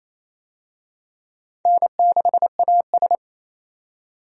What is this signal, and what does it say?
Morse code, heard as a keyed tone, about 28 wpm, 700 Hz. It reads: N6AS
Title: USA Call Signs - 4 Characters - 28wpm